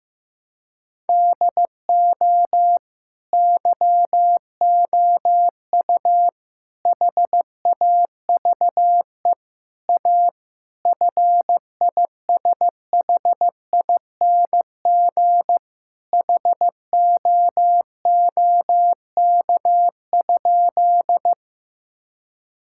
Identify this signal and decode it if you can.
Morse code, heard as a keyed tone, about 15 wpm, 700 Hz. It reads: DO YOU HAVE A FISHING HOOK?
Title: Sentences from Top 1600 Words 15wpm